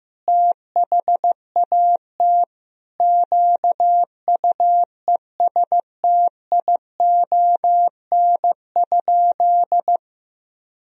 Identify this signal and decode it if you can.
Morse code, heard as a keyed tone, about 15 wpm, 700 Hz. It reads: THAT QUESTION?
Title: Sentences from Top 400 Words and Abbreviations 15wpm